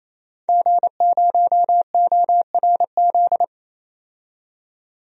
Morse code, heard as a keyed tone, about 30 wpm, 700 Hz. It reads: G0ORZ
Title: DXCC-100 Call Signs - 5 Characters - 28wpm